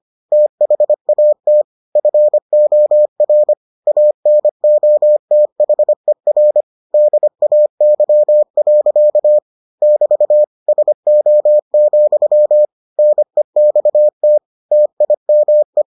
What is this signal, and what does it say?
Morse code, heard as a keyed tone, about 25 wpm, 600 Hz. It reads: THAT FOR ANOTHER DAY. <BT> SO, NEXT TIME